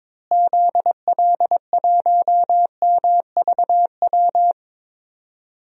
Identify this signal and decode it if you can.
Morse code, heard as a keyed tone, about 22 wpm, 700 Hz. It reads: ZL1MVW
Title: DXCC-100 Call Signs - 6 Characters - Repeated 3x - 22wpm